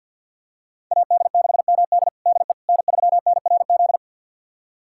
Morse code, heard as a keyed tone, about 53 wpm, 700 Hz. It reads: AD6MD DE N3NRZ
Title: USA Callsign DE Callsign - 5 Characters - 50wpm